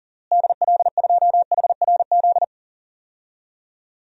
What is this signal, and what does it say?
Morse code, heard as a keyed tone, about 40 wpm, 700 Hz. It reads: DL2HRZ
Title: DXCC-100 Call Signs - 6 Characters - Repeated 3x - 40wpm